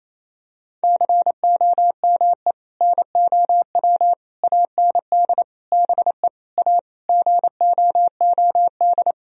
Morse code, heard as a keyed tone, about 28 wpm, 700 Hz. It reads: COME NOW AND BE A GOOD
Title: Sentences from Top 1000 Words - Review Part 4 - 28wpm